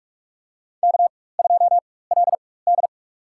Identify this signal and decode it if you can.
Morse code, heard as a keyed tone, about 46 wpm, spaced as for 21 wpm, 700 Hz. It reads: K2LD
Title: USA Call Signs - 4 Characters - 4x Character Spacing - 45wpm